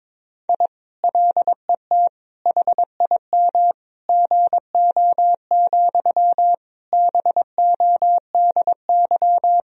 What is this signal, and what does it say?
Morse code, heard as a keyed tone, about 22 wpm, 700 Hz. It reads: I LET HIM GO, BODY
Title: Sentences from Top 1000 Words - Review Part 2 - 22wpm